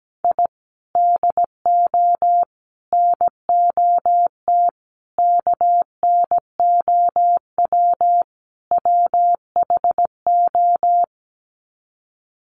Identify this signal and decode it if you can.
Morse code, heard as a keyed tone, about 17 wpm, 700 Hz. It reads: I DO NOT KNOW WHO
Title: Sets of 5 Words 17wpm